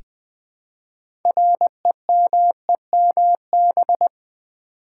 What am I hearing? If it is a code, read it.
Morse code, heard as a keyed tone, about 20 wpm, 700 Hz. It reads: REMEMB
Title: Build A Word - 2 Letters At A Time - 8-Letter Words - 20wpm